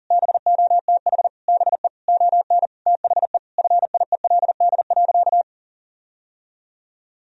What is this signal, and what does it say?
Morse code, heard as a keyed tone, about 40 wpm, 700 Hz. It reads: BOTH BE ON THE FIELD.
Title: Sentences from Top 1000 Words - Review Part 2 - 40wpm